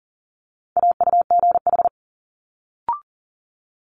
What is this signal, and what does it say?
Morse code, heard as a keyed tone, about 40 wpm, 700 Hz. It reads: AUGH
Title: Multi-letter Phonemes Quick Fire 40wpm